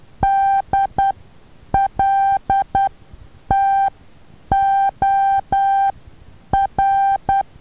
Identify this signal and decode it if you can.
Morse code, heard as a keyed tone, about 10 wpm, spaced as for 8 wpm, 790 Hz. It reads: DLTOR
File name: Taviro HA6VA B